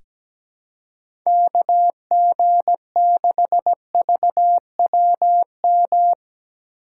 Morse code, heard as a keyed tone, about 17 wpm, 700 Hz. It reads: KG6VWM
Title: USA Call Signs - 6 Characters - 4x Character Spacing - 17wpm